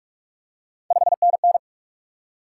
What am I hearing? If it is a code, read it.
Morse code, heard as a keyed tone, about 44 wpm, 700 Hz. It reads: 5NN
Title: Top 180 QSO Elements 45wpm